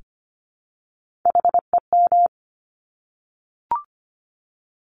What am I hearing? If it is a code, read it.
Morse code, heard as a keyed tone, about 25 wpm, 700 Hz. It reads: HEM
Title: Top Three Letter Combinations in Words 25wpm